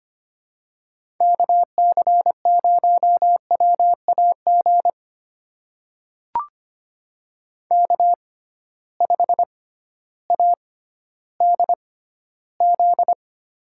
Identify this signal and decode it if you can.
Morse code, heard as a keyed tone, about 25 wpm, spaced as for 14 wpm, 700 Hz. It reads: KC0WAG K5ADZ
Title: USA Call Signs - 6x Character Spacing - 25wpm